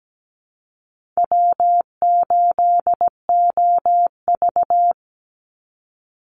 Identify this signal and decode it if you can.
Morse code, heard as a keyed tone, about 17 wpm, 700 Hz. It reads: W8OV
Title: Easy Call Signs Repeated 3x - 17wpm